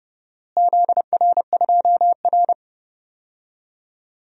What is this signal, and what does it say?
Morse code, heard as a keyed tone, about 31 wpm, 700 Hz. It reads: ZR2R
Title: DXCC-100 Call Signs - 4 Characters - Repeated 3x - 30wpm